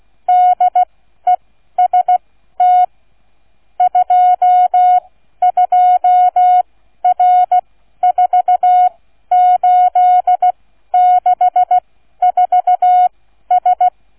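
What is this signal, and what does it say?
Morse code, heard as a keyed tone, about 16 wpm, spaced as for 12 wpm, 730 Hz. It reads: DEST 22R4864S
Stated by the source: Morse code receive practice, flight number, from ZL3TK.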